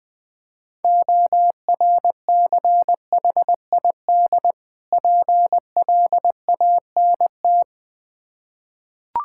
Morse code, heard as a keyed tone, about 21 wpm, 690 Hz. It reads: ORCHID PLANT
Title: Difficult Words with Hints - 20wpm